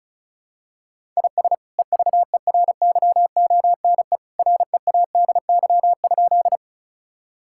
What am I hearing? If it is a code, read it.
Morse code, heard as a keyed tone, about 36 wpm, 700 Hz. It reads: IS EVERYONE READY?
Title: Sentences from Top 1000 Words - Review Part 5 - 35wpm